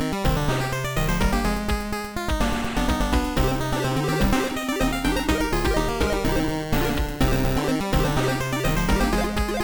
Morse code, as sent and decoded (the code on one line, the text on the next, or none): none